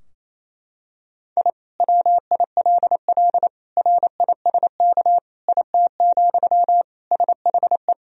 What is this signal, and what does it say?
Morse code, heard as a keyed tone, about 28 wpm, 700 Hz. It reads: I WILL RISK IT, SHE
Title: Sentences from Top 1000 Words - Review Part 4 - 28wpm